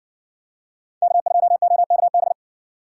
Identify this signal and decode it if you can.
Morse code, heard as a keyed tone, about 54 wpm, 700 Hz. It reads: K2OKB